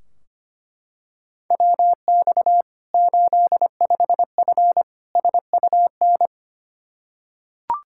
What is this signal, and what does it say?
Morse code, heard as a keyed tone, about 25 wpm, 700 Hz. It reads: WX 85F SUN
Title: Practice QSO - Exchange 2 - Info - 25wpm